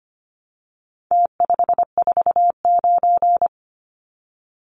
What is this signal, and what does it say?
Morse code, heard as a keyed tone, about 25 wpm, 700 Hz. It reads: T549
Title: Numbers - Contest Style - 4-Digit 25wpm